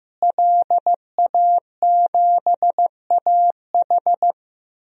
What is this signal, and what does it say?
Morse code, heard as a keyed tone, about 15 wpm, 700 Hz. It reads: LA7AH
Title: DXCC-100 Call Signs - 5 Characters - Repeated 2x - 15wpm